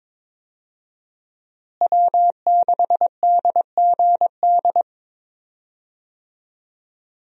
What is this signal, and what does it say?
Morse code, heard as a keyed tone, about 22 wpm, 700 Hz. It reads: W6DGD
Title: USA Call Signs - 5 Characters - Repeated 3x - 22wpm